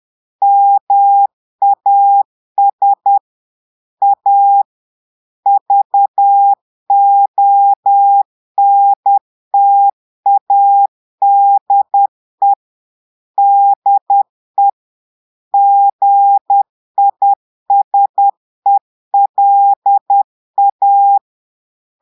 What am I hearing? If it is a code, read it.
Morse code, heard as a keyed tone, about 10 wpm, 800 Hz. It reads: MAS A VONTADE DE GISELA
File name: Portugais - texte en clair 1 - vitesse - 10 mots minute